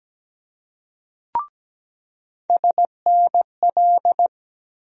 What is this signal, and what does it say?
Morse code, heard as a keyed tone, about 17 wpm, 700 Hz. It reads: SNL
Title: USA Call Sign Postfixes - 3 Letters 17wpm